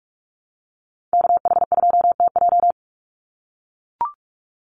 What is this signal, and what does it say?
Morse code, heard as a keyed tone, about 44 wpm, 700 Hz. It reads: KH2TJ
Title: NA SOTA Activator Callsigns 45wpm